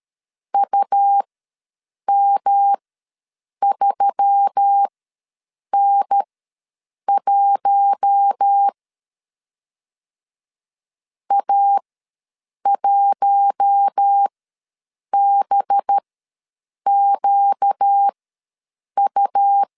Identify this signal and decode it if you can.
Morse code, heard as a keyed tone, about 13 wpm, spaced as for 7 wpm, 800 Hz. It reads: UM3N1 A1BQU
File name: LEZIONI / 6 PKX0 / 6 P ws 12wpm 3117